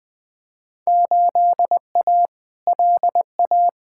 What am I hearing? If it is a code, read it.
Morse code, heard as a keyed tone, about 20 wpm, 700 Hz. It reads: 8A LA
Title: Field Day Exchanges Encore - 20wpm